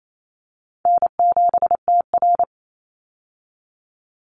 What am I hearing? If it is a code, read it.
Morse code, heard as a keyed tone, about 28 wpm, 700 Hz. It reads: N7TR
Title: Call Signs Repeated 2x - 28wpm